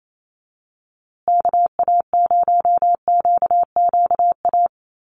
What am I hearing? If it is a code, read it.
Morse code, heard as a keyed tone, about 28 wpm, 700 Hz. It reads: KA0QQA
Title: USA Call Signs Repeated 3x 28wpm